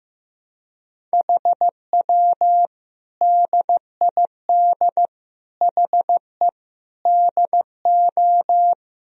Morse code, heard as a keyed tone, about 15 wpm, 700 Hz. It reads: HW DID HE DO
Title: Sentences from Top 100 Words and Abbreviations 15wpm